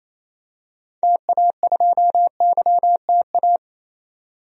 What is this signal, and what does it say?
Morse code, heard as a keyed tone, about 28 wpm, 700 Hz. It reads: TA2YTA
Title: DXCC-100 Call Signs - 6 Characters - 28wpm